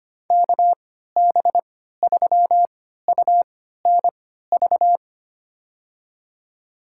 Morse code, heard as a keyed tone, about 25 wpm, spaced as for 14 wpm, 700 Hz. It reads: KB3UNV
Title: USA Call Signs - 6 Characters - 3x Character Spacing - 25wpm